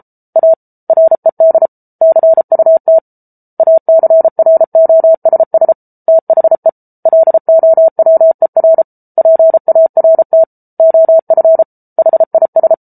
Morse code, heard as a keyed tone, about 34 wpm, 650 Hz. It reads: A RED CUT ACROSS THE LOWER PART OF HIS